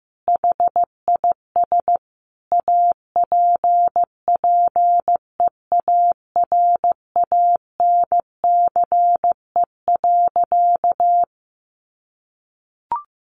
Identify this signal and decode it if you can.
Morse code, heard as a keyed tone, about 14 wpm, 700 Hz. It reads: HIS APPEARANCE.
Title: Sentences from Top 1100 Words 15wpm